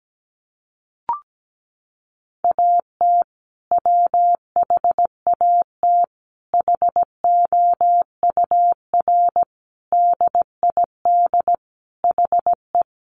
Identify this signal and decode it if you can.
Morse code, heard as a keyed tone, about 17 wpm, 700 Hz. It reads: AT WHAT HOUR DID HE
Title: Sentences from Top 700 Words 17wpm